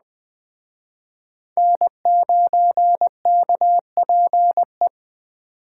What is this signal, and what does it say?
Morse code, heard as a keyed tone, about 20 wpm, 700 Hz. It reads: N9KPE
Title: Speed Racing - USA 1x3 Call Signs 17wpm